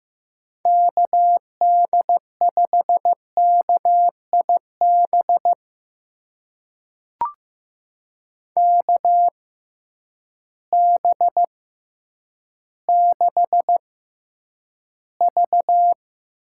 Morse code, heard as a keyed tone, about 15 wpm, spaced as for 9 wpm, 700 Hz. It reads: KD5KIB KB6V
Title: USA Call Signs - 6 Characters - 6x Character Spacing - 15wpm